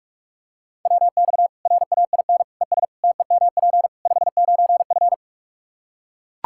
Morse code, heard as a keyed tone, about 41 wpm, 700 Hz. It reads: WX RAIN ES TEMP 59F
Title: Practice QSO - Exchange 2 - Info - 45wpm